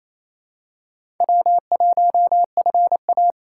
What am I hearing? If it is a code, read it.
Morse code, heard as a keyed tone, about 28 wpm, 700 Hz. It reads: W1FA
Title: USA Call Signs - 2x Character Spacing - 28wpm